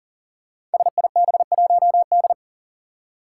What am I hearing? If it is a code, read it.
Morse code, heard as a keyed tone, about 40 wpm, 700 Hz. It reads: SIB1D
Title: Five Challenging Letter-Number Combinations 40wpm